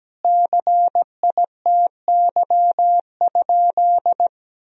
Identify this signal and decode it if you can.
Morse code, heard as a keyed tone, about 17 wpm, 700 Hz. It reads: CITY?
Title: Sentences from Top 1000 Words - Review Part 1 - 17wpm